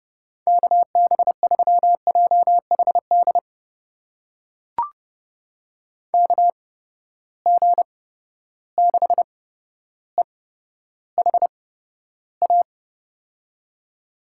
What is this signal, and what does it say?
Morse code, heard as a keyed tone, about 31 wpm, spaced as for 13 wpm, 700 Hz. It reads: KB3JHD KG6EHA
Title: USA Call Signs - 6 Characters - 8x Character Spacing - 30wpm